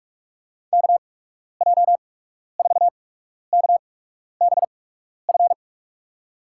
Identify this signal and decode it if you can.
Morse code, heard as a keyed tone, about 45 wpm, spaced as for 12 wpm, 700 Hz. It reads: KJ4KBF
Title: USA Call Signs - 8x Character Spacing - 45wpm